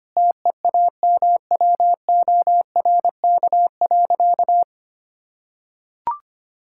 Morse code, heard as a keyed tone, about 25 wpm, 700 Hz. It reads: TEAMWORK.
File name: practice / 25wpm / Difficult Sentences with Hints - 25wpm